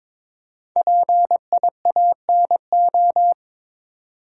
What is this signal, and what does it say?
Morse code, heard as a keyed tone, about 22 wpm, 700 Hz. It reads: PIANO
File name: 5 Letter Words - First 3 Letters Given - 22wpm